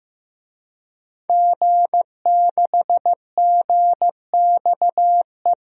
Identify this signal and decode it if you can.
Morse code, heard as a keyed tone, about 15 wpm, 700 Hz. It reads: G6GXE
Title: DXCC-100 Call Signs - 5 Characters - 15wpm